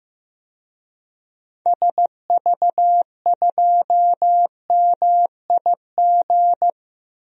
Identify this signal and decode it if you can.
Morse code, heard as a keyed tone, about 15 wpm, 700 Hz. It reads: SV2MIG
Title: DXCC-100 Call Signs - 6 Characters - Repeated 3x - 15wpm